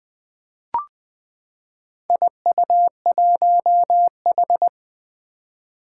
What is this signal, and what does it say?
Morse code, heard as a keyed tone, about 20 wpm, 700 Hz. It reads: IU1H
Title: Four Challenging Letter-Number Combinations 20wpm